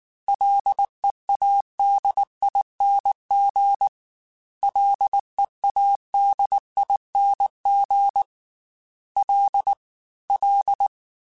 Morse code, heard as a keyed tone, about 19 wpm, 800 Hz. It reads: LEADING LEADING L L
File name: F and L / F and L mp3 / F and L Prefixes 3x2 Twice @18